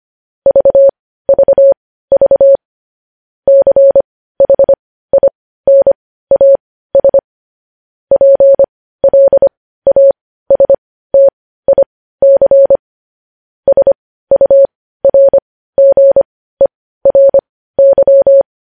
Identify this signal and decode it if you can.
Morse code, heard as a keyed tone, about 25 wpm, spaced as for 15 wpm, 550 Hz. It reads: VVV CHINAS PLASTIC SURGERY
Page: News Headlines in Morse Code at 15 WPM